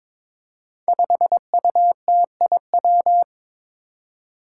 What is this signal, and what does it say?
Morse code, heard as a keyed tone, about 21 wpm, 700 Hz. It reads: 5UTIW
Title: Five Challenging Letter-Number Combinations 22wpm